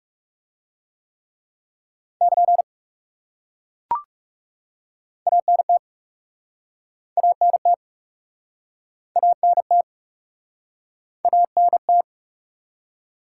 Morse code, heard as a keyed tone, about 37 wpm, 700 Hz. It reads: <KN> ANT ANT ANT ANT
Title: Speed-Racing Top 180 QSO Elements 30wpm